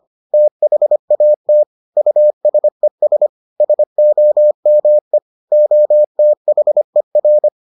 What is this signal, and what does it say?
Morse code, heard as a keyed tone, about 25 wpm, 600 Hz. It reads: THAT USES SOME OTHER